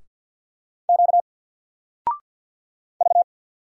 Morse code, heard as a keyed tone, about 50 wpm, 700 Hz. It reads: Q V
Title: Single Letter-Number 50wpm